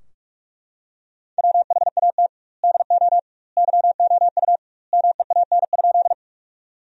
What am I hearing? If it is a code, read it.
Morse code, heard as a keyed tone, about 41 wpm, 700 Hz. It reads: WHAT DO YOU MEAN?
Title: 2x Word-Spacing - Sentences from Top 200 Words - 45wpm